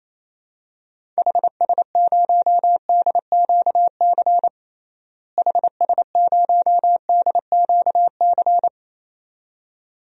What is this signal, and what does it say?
Morse code, heard as a keyed tone, about 28 wpm, 700 Hz. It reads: HS0DQC HS0DQC
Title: DXCC-100 Call Signs - 6 Characters - Repeated 2x - 28wpm